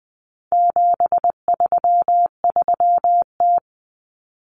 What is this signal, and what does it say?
Morse code, heard as a keyed tone, about 20 wpm, 700 Hz. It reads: 733T
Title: Common Cut Numbers - 4-Digit 20wpm